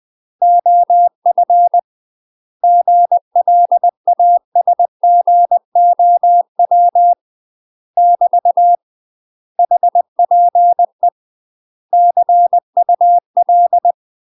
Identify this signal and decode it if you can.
Morse code, heard as a keyed tone, about 20 wpm, 700 Hz. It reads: OF GLASGOW <BT> HPE CUL
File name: UK MASTER QSOs TRAINING / 20WPM / UK MASTER QSOs 0003